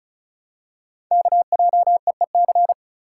Morse code, heard as a keyed tone, about 33 wpm, 700 Hz. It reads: KJEEC
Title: Code Groups - 5 Letters - 35wpm